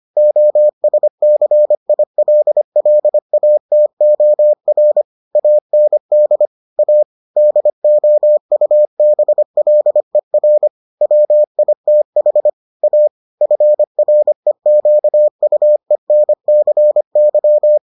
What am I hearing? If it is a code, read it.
Morse code, heard as a keyed tone, about 25 wpm, 600 Hz. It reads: OSCILLATOR AND A DOUBLER WITH A FREQUENCY